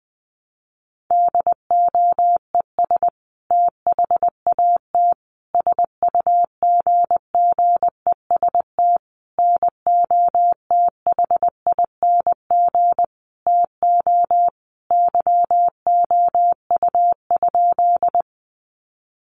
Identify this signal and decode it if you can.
Morse code, heard as a keyed tone, about 20 wpm, 700 Hz. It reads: DOES THAT SUGGEST NOTHING TO YOU?
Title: Sentences from Top 1300 Words 20wpm